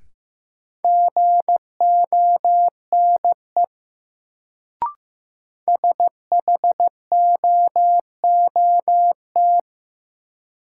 Morse code, heard as a keyed tone, about 15 wpm, 700 Hz. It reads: GONE SHOOT
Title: Sentences from Top 1200 Words - New Word Drill Alt - 15wpm